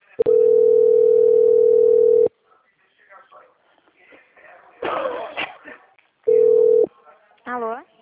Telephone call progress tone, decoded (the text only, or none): ringback tone